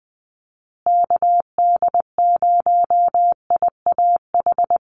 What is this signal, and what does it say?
Morse code, heard as a keyed tone, about 20 wpm, 700 Hz. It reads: KD0IAH
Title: USA Call Signs Repeated 2x 20wpm